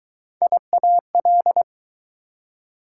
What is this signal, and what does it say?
Morse code, heard as a keyed tone, about 20 wpm, 700 Hz. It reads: IAL